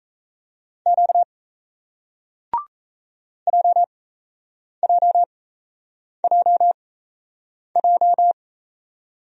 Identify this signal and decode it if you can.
Morse code, heard as a keyed tone, about 38 wpm, spaced as for 8 wpm, 700 Hz. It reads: Q JJJJ